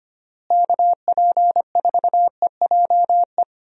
Morse code, heard as a keyed tone, about 25 wpm, 700 Hz. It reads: KP4EJE
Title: DXCC-100 Call Signs - 6 Characters - Repeated 3x - 25wpm